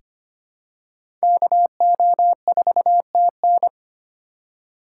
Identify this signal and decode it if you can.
Morse code, heard as a keyed tone, about 25 wpm, 700 Hz. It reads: KO4TN
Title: USA Call Signs - 5 Characters - 3x Character Spacing - 25wpm